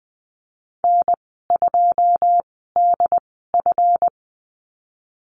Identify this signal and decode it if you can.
Morse code, heard as a keyed tone, about 20 wpm, spaced as for 15 wpm, 700 Hz. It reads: N2DF
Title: 186 Lesson 30g - Letter b - Call Signs 20wpm